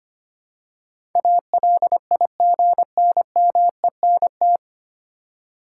Morse code, heard as a keyed tone, about 25 wpm, 700 Hz. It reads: ALIGNMENT
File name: Nine Letter Words - 25wpm